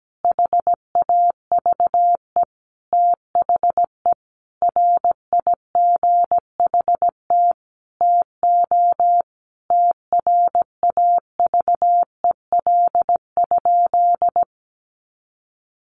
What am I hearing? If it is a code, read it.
Morse code, heard as a keyed tone, about 17 wpm, 700 Hz. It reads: HAVE THE RIGHT TO TRAVEL?
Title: Sentences from Top 1400 Words 17wpm